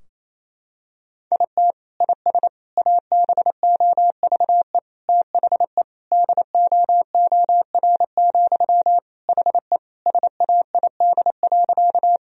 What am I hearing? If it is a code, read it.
Morse code, heard as a keyed tone, about 28 wpm, 700 Hz. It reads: IT IS ABOVE THE DOOR, HE SAID.
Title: Sentences from Top 1000 Words - Review Part 2 - 28wpm